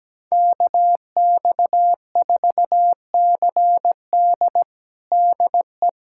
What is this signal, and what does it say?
Morse code, heard as a keyed tone, about 17 wpm, 700 Hz. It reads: KX4CD DE
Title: USA Callsign DE Callsign - 5 Characters - 17wpm